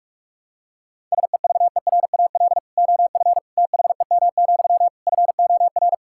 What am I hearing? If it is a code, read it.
Morse code, heard as a keyed tone, about 46 wpm, 690 Hz. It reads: SEVERAL OF THEM, FOR
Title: Sentences from Top 1000 Words - Review Part 5 - 45wpm